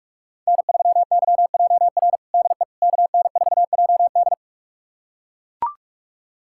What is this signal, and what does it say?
Morse code, heard as a keyed tone, about 42 wpm, 700 Hz. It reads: N3YJR DE KN4JD
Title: USA Callsign DE Callsign - 5 Characters - 45wpm